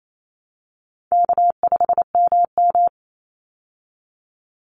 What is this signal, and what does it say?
Morse code, heard as a keyed tone, about 28 wpm, 700 Hz. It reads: K5MM
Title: Easy Call Signs Repeated 3x - 28wpm